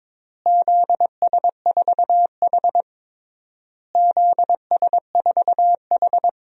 Morse code, heard as a keyed tone, about 22 wpm, 700 Hz. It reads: ZS4H ZS4H
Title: DXCC-100 Call Signs - 4 Characters - Repeated 3x - 22wpm